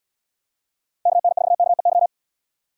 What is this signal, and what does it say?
Morse code, heard as a keyed tone, about 76 wpm, 700 Hz. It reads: KN4KEY